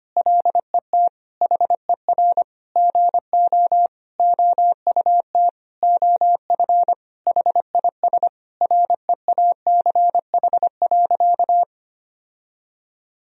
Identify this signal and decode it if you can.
Morse code, heard as a keyed tone, about 25 wpm, 700 Hz. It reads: LET HER GO OUT OF HIS REACH.
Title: Sentences from Top 1000 Words - Review Part 2 - 25wpm